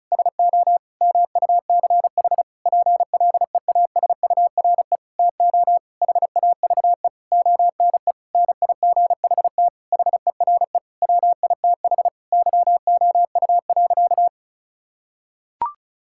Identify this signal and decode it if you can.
Morse code, heard as a keyed tone, about 36 wpm, 700 Hz. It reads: SO MUCH PLEASURE TO HAVE ONE NIGHT HERE WITH YOU.
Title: Sentences from Top 1600 Words 35wpm